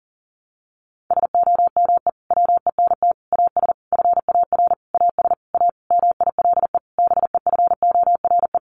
Morse code, heard as a keyed tone, about 40 wpm, 700 Hz. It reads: SOME WENT AS FAR AS A MILE BEFORE
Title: Sentences from Top 1100 Words 40wpm